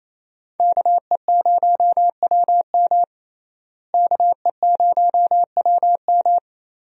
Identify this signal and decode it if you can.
Morse code, heard as a keyed tone, about 28 wpm, 700 Hz. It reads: KE0WM KE0WM
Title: USA Call Signs - 5 Characters - Repeated 3x - 28wpm